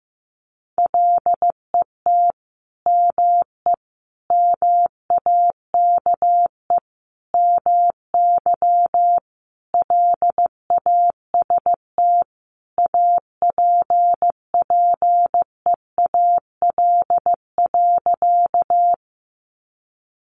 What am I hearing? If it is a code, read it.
Morse code, heard as a keyed tone, about 15 wpm, 700 Hz. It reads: LET ME MAKE MY LAST APPEAL.
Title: Sentences from Top 1300 Words 15wpm